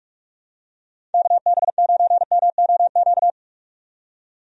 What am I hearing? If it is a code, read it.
Morse code, heard as a keyed tone, about 46 wpm, 700 Hz. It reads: KB9MOQ
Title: USA Call Signs - 6 Characters - 45wpm